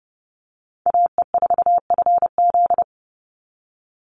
Morse code, heard as a keyed tone, about 30 wpm, 700 Hz. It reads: AE4FZ
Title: NA SOTA Chaser Callsigns 30wpm